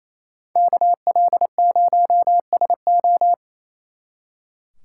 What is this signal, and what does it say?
Morse code, heard as a keyed tone, about 33 wpm, 700 Hz. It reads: KL0SO
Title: Speed Racing - USA 2x2 Call Signs 28wpm